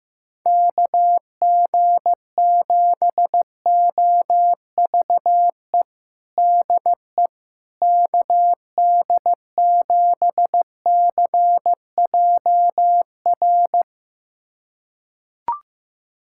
Morse code, heard as a keyed tone, about 15 wpm, 700 Hz. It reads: KG7OVE DE KD7CJR
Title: USA Callsign DE Callsign 15wpm